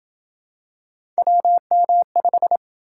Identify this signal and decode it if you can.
Morse code, heard as a keyed tone, about 27 wpm, 700 Hz. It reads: WM5